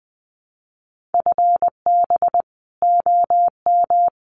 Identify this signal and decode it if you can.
Morse code, heard as a keyed tone, about 20 wpm, 700 Hz. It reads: FB OM
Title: Common QSO Phrases 20wpm